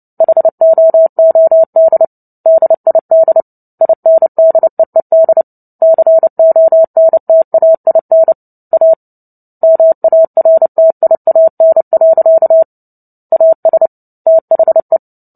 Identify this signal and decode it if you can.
Morse code, heard as a keyed tone, about 30 wpm, 650 Hz. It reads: HOOD DID INDEED CONTAIN A MARTIAN. AS THE
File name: Book Two / 03-Chapter 3 - 29 WPM